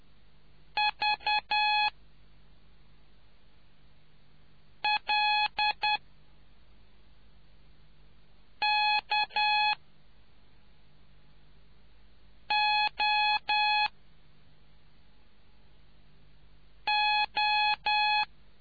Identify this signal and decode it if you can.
Morse code, heard as a keyed tone, about 10 wpm, spaced as for 3 wpm, 840 Hz. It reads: VLKOO